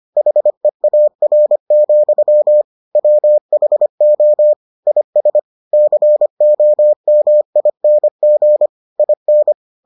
Morse code, heard as a keyed tone, about 25 wpm, 600 Hz. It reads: HEAR, WHO IS COMING IN